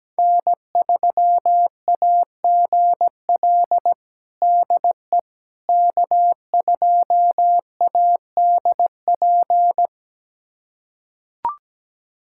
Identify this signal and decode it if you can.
Morse code, heard as a keyed tone, about 17 wpm, 700 Hz. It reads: N3AGL DE K2ADP
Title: USA Callsign DE Callsign - 5 Characters - 17wpm